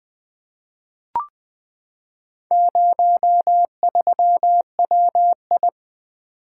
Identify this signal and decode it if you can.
Morse code, heard as a keyed tone, about 20 wpm, 700 Hz. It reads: 03WI
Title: Four Letter-Number Combinations - 20wpm